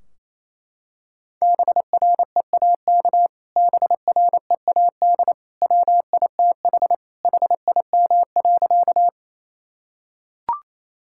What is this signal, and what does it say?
Morse code, heard as a keyed tone, about 28 wpm, 700 Hz. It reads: BREAK BREAD WITH HIM.
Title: Sentences from Top 1600 Words 28wpm